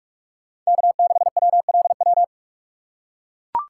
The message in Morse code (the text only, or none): K6WLW